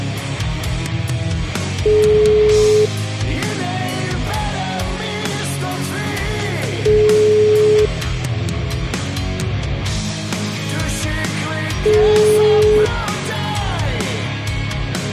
Ringback tone (Europe, 425 Hz), with noise.